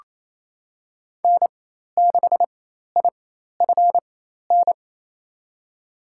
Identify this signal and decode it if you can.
Morse code, heard as a keyed tone, about 28 wpm, spaced as for 13 wpm, 700 Hz. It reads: N6IFN
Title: USA Call Signs - 5 Characters - 4x Character Spacing - 28wpm